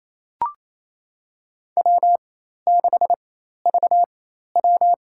Morse code, heard as a keyed tone, about 27 wpm, spaced as for 13 wpm, 700 Hz. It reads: W6VW
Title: USA Call Signs - 4x Character Spacing - 28wpm